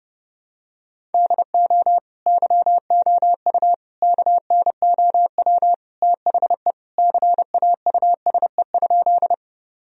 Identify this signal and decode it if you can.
Morse code, heard as a keyed tone, about 29 wpm, 700 Hz. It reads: DO YOU KNOW THE CAUSE?
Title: Sentences from Top 1000 Words - Review Part 1 - 30wpm